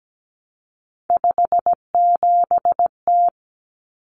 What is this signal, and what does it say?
Morse code, heard as a keyed tone, about 17 wpm, 700 Hz. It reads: 57T
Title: Common Cut Numbers - 3-Digit 17wpm